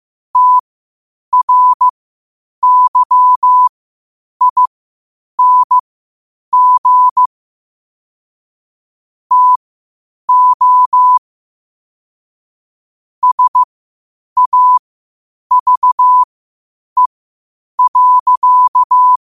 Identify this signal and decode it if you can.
Morse code, heard as a keyed tone, about 15 wpm, spaced as for 8 wpm, 1000 Hz. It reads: TRYING TO SAVE.
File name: qotdfiles / 8wpm